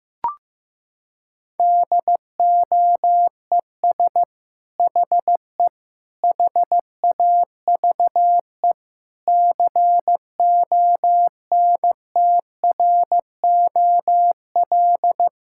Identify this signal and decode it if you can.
Morse code, heard as a keyed tone, about 15 wpm, 700 Hz. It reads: DOES HE HAVE CONTROL
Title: Sentences from Top 1000 Words - Review Part 5 - 15wpm